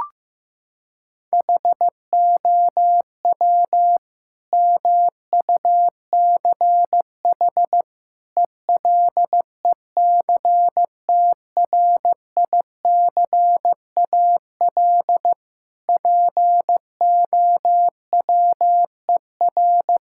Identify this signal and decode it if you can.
Morse code, heard as a keyed tone, about 14 wpm, 700 Hz. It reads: HOW MUCH ELECTRICAL POWER
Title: Sentences from Top 1000 Words - Review Part 6 - 15wpm